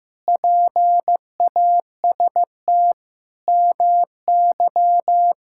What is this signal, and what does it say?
Morse code, heard as a keyed tone, about 15 wpm, 700 Hz. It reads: PAST MY
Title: Pangram Challenge v2 - 15wpm